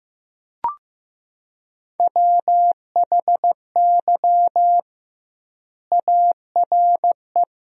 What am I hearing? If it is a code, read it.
Morse code, heard as a keyed tone, about 15 wpm, 700 Hz. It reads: WHY ARE
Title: 2x Word-Spacing - Sentences from Top 200 Words - 15wpm